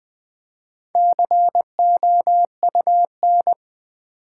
Morse code, heard as a keyed tone, about 20 wpm, 700 Hz. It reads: COUN
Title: Build A Word - 4 Letters At A Time - 11-Letter Words - 20wpm